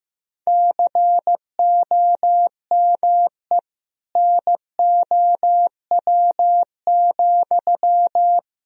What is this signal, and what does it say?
Morse code, heard as a keyed tone, about 15 wpm, 700 Hz. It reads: COME NOW,
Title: Sentences from Top 1000 Words - Review Part 2 - 15wpm